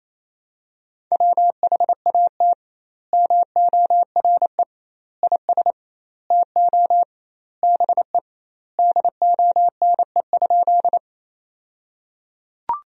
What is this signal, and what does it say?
Morse code, heard as a keyed tone, about 28 wpm, 700 Hz. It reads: WHAT MORE IS TO BE DONE?
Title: 2x Word-Spacing - Sentences from Top 500 Words - 28wpm